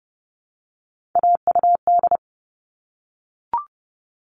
Morse code, heard as a keyed tone, about 30 wpm, 700 Hz. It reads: AUD
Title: All Cut Numbers - 3-Digit 30wpm